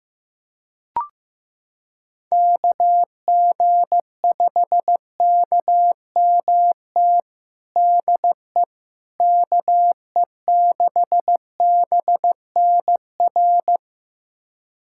Morse code, heard as a keyed tone, about 16 wpm, 700 Hz. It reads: KG5KMT DE KE6BNR
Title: USA Callsign DE Callsign - 6 Characters - 15wpm